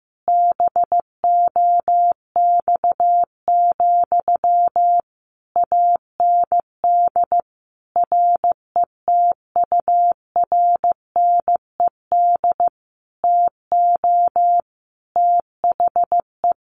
Morse code, heard as a keyed tone, about 14 wpm, 700 Hz. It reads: BOX, AND RETURNED TO THE
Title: A Sound Of Thunder 15wpm